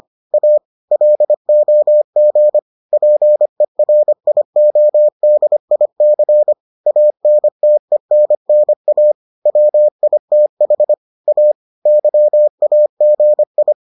Morse code, heard as a keyed tone, about 25 wpm, 600 Hz. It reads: A LOG PERIODIC ANTENNA WITH A YAGI